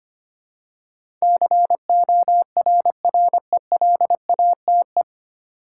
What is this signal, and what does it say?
Morse code, heard as a keyed tone, about 25 wpm, 700 Hz. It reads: CORRELATE
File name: Nine Letter Words - 25wpm